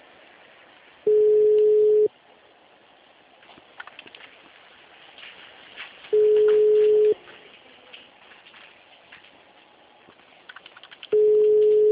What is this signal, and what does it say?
Ringback tone (Europe, 425 Hz).
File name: out- -2022- - - .17257